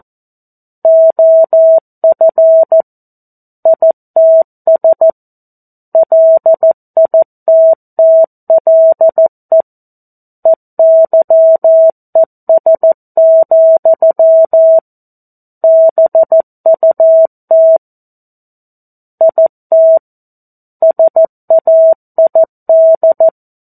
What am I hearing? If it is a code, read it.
Morse code, heard as a keyed tone, about 14 wpm, 650 Hz. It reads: OF ITS LITTLE EYES, BUT IT SAID
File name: 02-Chapter-02 14 WPM